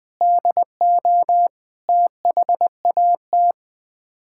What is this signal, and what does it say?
Morse code, heard as a keyed tone, about 21 wpm, 700 Hz. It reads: DO THAT
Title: Build A Phrase - 5-Word Phrases - 20wpm